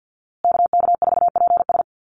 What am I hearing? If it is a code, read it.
Morse code, heard as a keyed tone, about 50 wpm, 700 Hz. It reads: KK4PS
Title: USA Call Signs Repeated 3x 50wpm